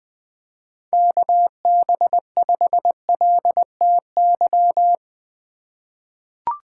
Morse code, heard as a keyed tone, about 20 wpm, 700 Hz. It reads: KB5LTY
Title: USA Call Signs - 4x Character Spacing - 20wpm